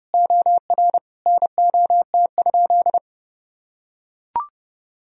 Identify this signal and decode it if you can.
Morse code, heard as a keyed tone, about 30 wpm, 700 Hz. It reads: OR NOT?
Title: Sentences from Top 1000 Words - Review Part 3 - 30wpm